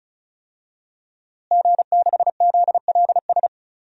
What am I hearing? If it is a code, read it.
Morse code, heard as a keyed tone, about 36 wpm, 690 Hz. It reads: G6ZLS
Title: DXCC-100 Call Signs - 5 Characters - Repeated 3x - 35wpm